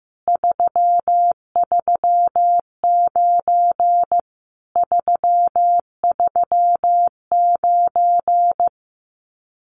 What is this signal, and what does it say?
Morse code, heard as a keyed tone, about 15 wpm, 700 Hz. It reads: 339 339
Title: Common QSO Phrases Encore 15wpm